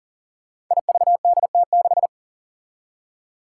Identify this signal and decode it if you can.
Morse code, heard as a keyed tone, about 40 wpm, 700 Hz. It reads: IVDT6
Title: Five Challenging Letter-Number Combinations 40wpm